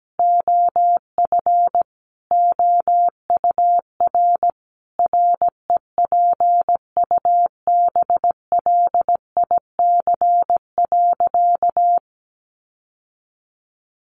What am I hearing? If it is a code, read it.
Morse code, heard as a keyed tone, about 17 wpm, 700 Hz. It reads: OF OUR REPUBLIC.
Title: Sentences from Top 1500 Words 17wpm